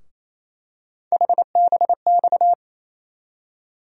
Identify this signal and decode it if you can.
Morse code, heard as a keyed tone, about 28 wpm, 700 Hz. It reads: HBX
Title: USA Call Sign Postfixes - 3 Letters 28wpm